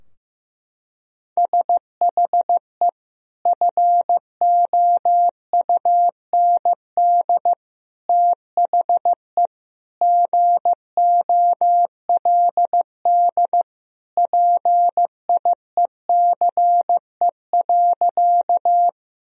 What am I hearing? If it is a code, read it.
Morse code, heard as a keyed tone, about 15 wpm, 700 Hz. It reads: SHE FOUND THE GOLD PIECE.
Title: Sentences from Top 1000 Words - Review Part 3 - 15wpm